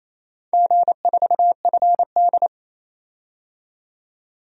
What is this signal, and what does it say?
Morse code, heard as a keyed tone, about 28 wpm, 700 Hz. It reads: G4FD
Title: DXCC-100 Call Signs - 4 Characters - 28wpm